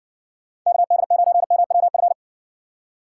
Morse code, heard as a keyed tone, about 60 wpm, 700 Hz. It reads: KD9MWF